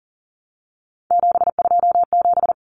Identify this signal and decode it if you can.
Morse code, heard as a keyed tone, about 40 wpm, 700 Hz. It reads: 727
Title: Numbers - 3-Digit 40wpm